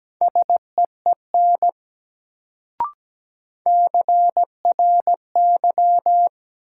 Morse code, heard as a keyed tone, about 17 wpm, 700 Hz. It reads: SEEN CRY
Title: Sets of 5 Words Encore - New Word Drill - 17wpm